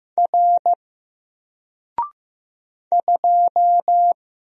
Morse code, heard as a keyed tone, about 15 wpm, 700 Hz. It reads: R 2
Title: Single Letter-Number 15wpm